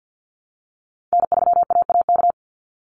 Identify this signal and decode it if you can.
Morse code, heard as a keyed tone, about 48 wpm, 700 Hz. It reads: N3AAK
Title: Call Signs Repeated 3x - 50wpm